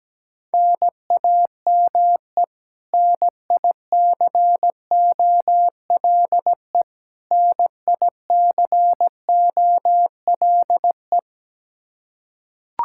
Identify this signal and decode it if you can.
Morse code, heard as a keyed tone, about 17 wpm, 700 Hz. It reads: NAME NICOLE NICOLE
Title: Practice QSO - Exchange 1 - Beginning - 17wpm